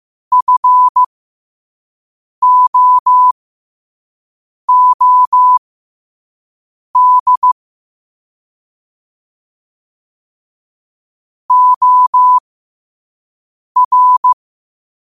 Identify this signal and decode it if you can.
Morse code, heard as a keyed tone, about 15 wpm, spaced as for 5 wpm, 1000 Hz. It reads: FOOD OR
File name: qotdfiles / 5wpm